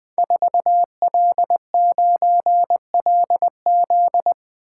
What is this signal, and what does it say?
Morse code, heard as a keyed tone, about 20 wpm, 700 Hz. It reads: 4L9LZ
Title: Code Groups - 5 Letter:Numbers - 20wpm